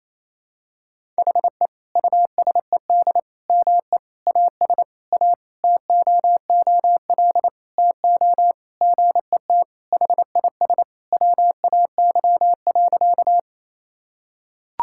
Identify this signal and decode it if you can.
Morse code, heard as a keyed tone, about 28 wpm, 700 Hz. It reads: HE USED ME AS A TOOL TO GET HIS WAY.
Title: Sentences from Top 1000 Words - Review Part 6 - 28wpm